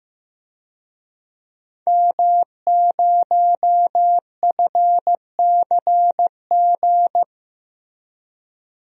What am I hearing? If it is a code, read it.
Morse code, heard as a keyed tone, about 15 wpm, 700 Hz. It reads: M0FCG
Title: DXCC-100 Call Signs - 5 Characters - Repeated 2x - 15wpm